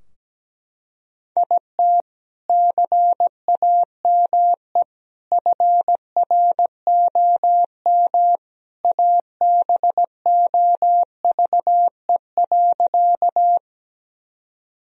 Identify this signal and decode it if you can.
Morse code, heard as a keyed tone, about 17 wpm, 700 Hz. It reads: IT CAME FROM ABOVE.
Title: Sentences from Top 1000 Words - Review Part 2 - 17wpm